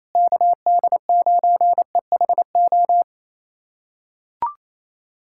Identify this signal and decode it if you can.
Morse code, heard as a keyed tone, about 28 wpm, 700 Hz. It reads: KD9EHO
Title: USA Callsign DE Callsign 28wpm